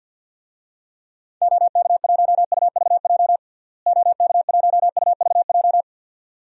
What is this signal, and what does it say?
Morse code, heard as a keyed tone, about 46 wpm, 700 Hz. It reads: OK1UVJ OK1UVJ
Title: DXCC-100 Call Signs - 6 Characters - Repeated 2x - 50wpm